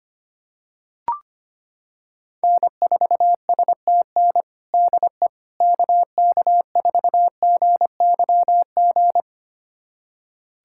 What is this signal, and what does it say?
Morse code, heard as a keyed tone, about 25 wpm, 700 Hz. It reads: N4STN DE KK4GYG
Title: USA Callsign DE Callsign 25wpm